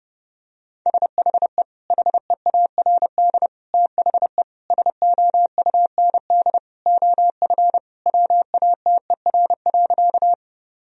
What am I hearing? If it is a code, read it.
Morse code, heard as a keyed tone, about 32 wpm, 700 Hz. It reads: SHE HEARD THE SOUND OF WATER.
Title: Sentences from Top 1000 Words - Review Part 1 - 30wpm